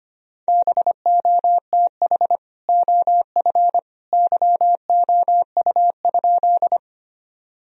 Morse code, heard as a keyed tone, about 25 wpm, 700 Hz. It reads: BOTH OF YOU?
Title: Sentences from Top 1000 Words - Review Part 1 - 25wpm